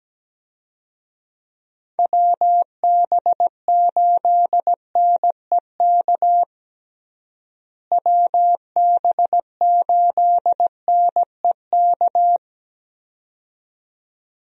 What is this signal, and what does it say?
Morse code, heard as a keyed tone, about 17 wpm, 700 Hz. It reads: WB8NEK WB8NEK
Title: USA Call Signs - 6 Characters - Repeated 2x - 17wpm